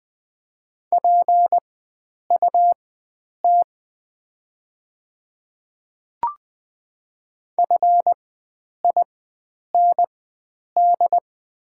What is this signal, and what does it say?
Morse code, heard as a keyed tone, about 20 wpm, spaced as for 9 wpm, 690 Hz. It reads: PUT FIND
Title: 4x Character Spacing - Top 300 Words - 20wpm